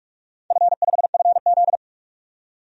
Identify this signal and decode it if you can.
Morse code, heard as a keyed tone, about 44 wpm, 700 Hz. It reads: F5FZ
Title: DXCC-100 Call Signs - 4 Characters - 45wpm